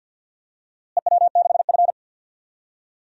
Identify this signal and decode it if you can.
Morse code, heard as a keyed tone, about 52 wpm, 700 Hz. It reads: EW6F
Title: DXCC-100 Call Signs - 4 Characters - Repeated 2x - 50wpm